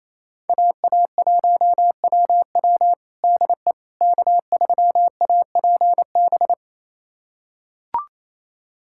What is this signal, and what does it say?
Morse code, heard as a keyed tone, about 28 wpm, 700 Hz. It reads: AA1WW DE K3APB
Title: USA Callsign DE Callsign - 5 Characters - 28wpm